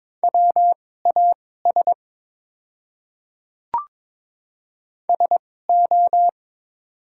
Morse code, heard as a keyed tone, about 22 wpm, spaced as for 16 wpm, 700 Hz. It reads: WAS SO
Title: 2x Character Spacing - Top 500 Words - 22wpm